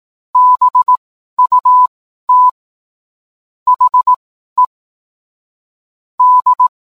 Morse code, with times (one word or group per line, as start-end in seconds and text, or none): 0.3-2.5 BUT
3.7-4.7 HE
6.2-6.7 D